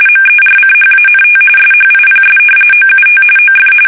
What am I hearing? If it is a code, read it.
RTTY, about 45.45 baud. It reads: MERCHANT SHIPS OUT OF
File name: fsk850 / usnavy2